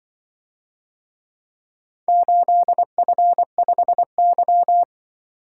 Morse code, heard as a keyed tone, about 22 wpm, 700 Hz. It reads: 8F5Y
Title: Speed-Racing - Four Letter-Number Combinations - 20wpm